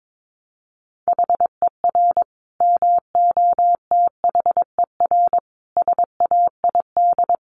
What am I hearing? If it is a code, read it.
Morse code, heard as a keyed tone, about 22 wpm, 700 Hz. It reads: HER MOTHER SAID
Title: Sentences from Top 1200 Words 22wpm